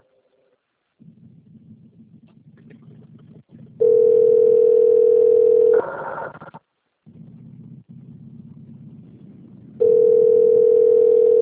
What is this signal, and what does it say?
Ringback tone (North America), 440 + 480 Hz.
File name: rg-1600-unknown- - - .54022